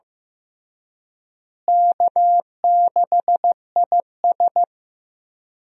Morse code, heard as a keyed tone, about 16 wpm, 700 Hz. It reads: K6IS
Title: USA Call Signs - 4 Characters - Repeated 3x - 15wpm